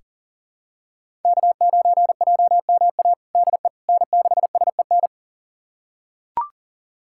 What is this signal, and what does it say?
Morse code, heard as a keyed tone, about 40 wpm, 700 Hz. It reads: K9JMA DE N6SEN
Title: USA Callsign DE Callsign - 5 Characters - 40wpm